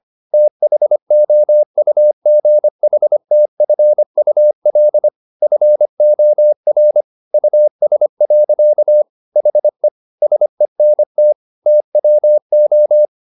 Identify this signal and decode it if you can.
Morse code, heard as a keyed tone, about 25 wpm, 600 Hz. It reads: THOUGHTFUL FOR US. HE SENT TWO
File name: the old man and the sea part 1 25wpm